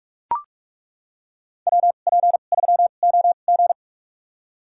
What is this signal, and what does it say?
Morse code, heard as a keyed tone, about 41 wpm, spaced as for 31 wpm, 690 Hz. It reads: WP3OG
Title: USA Call Signs - 5 Characters - 2x Character Spacing - 45wpm